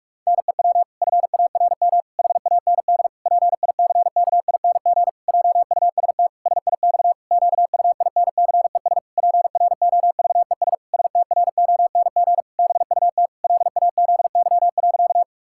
Morse code, heard as a keyed tone, about 46 wpm, 700 Hz. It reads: NEW FARM HAND PICKING JUST SIX QUINCES PROVES STRONG BUT LAZY.
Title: Pangram Challenge v2 - 45wpm